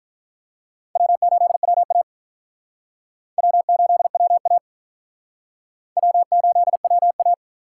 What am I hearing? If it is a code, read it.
Morse code, heard as a keyed tone, about 48 wpm, 700 Hz. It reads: W8WA W8WA W8WA